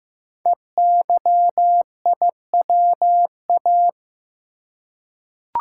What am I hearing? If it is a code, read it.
Morse code, heard as a keyed tone, about 15 wpm, 700 Hz. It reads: EYIWA
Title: Code Groups - 5 Letters - 15wpm